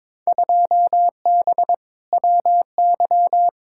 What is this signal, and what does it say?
Morse code, heard as a keyed tone, about 22 wpm, 700 Hz. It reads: 2B WY
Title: Field Day Exchanges Encore - 22wpm